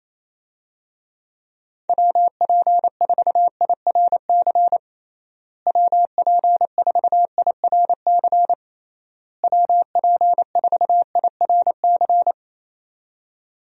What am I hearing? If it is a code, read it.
Morse code, heard as a keyed tone, about 28 wpm, 700 Hz. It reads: WP4IRC WP4IRC WP4IRC
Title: USA Call Signs - 6 Characters - Repeated 3x - 28wpm